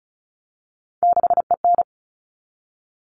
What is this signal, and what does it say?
Morse code, heard as a keyed tone, about 34 wpm, 700 Hz. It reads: 6EN
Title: All Cut Numbers - 3-Digit 35wpm